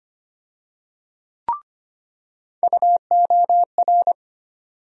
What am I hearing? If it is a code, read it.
Morse code, heard as a keyed tone, about 25 wpm, 700 Hz. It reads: UOR
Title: USA Call Sign Postfixes - 3 Letters 25wpm